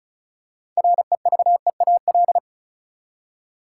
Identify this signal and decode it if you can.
Morse code, heard as a keyed tone, about 36 wpm, 700 Hz. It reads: REVEAL
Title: Six Letter Words - 35wpm